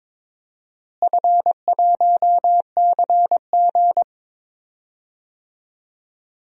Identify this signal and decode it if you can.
Morse code, heard as a keyed tone, about 22 wpm, 700 Hz. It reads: F1CG
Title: DXCC-100 Call Signs - 4 Characters - Repeated 2x - 22wpm